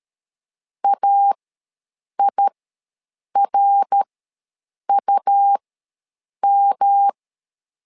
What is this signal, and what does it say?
Morse code, heard as a keyed tone, about 13 wpm, spaced as for 7 wpm, 800 Hz. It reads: AIRUM
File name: LEZIONI / 3 MWRO / 3 MWRO ws 12wpm 1914